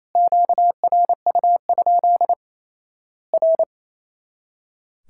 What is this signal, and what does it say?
Morse code, heard as a keyed tone, about 27 wpm, 700 Hz. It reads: QRU? R
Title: Worked All Europe DX Contest - QTC Exchange - 28wpm